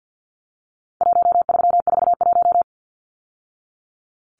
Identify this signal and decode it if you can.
Morse code, heard as a keyed tone, about 50 wpm, 700 Hz. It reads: 1341
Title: Numbers - 4-Digit 50wpm